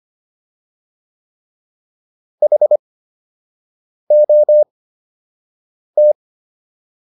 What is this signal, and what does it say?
Morse code, heard as a keyed tone, about 25 wpm, spaced as for 6 wpm, 600 Hz. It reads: HOT